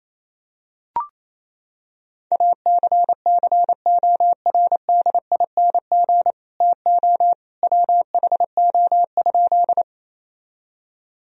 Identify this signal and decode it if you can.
Morse code, heard as a keyed tone, about 27 wpm, 700 Hz. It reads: ACCORDING TO WHO?
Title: Sentences from Top 1000 Words - Review Part 4 - 28wpm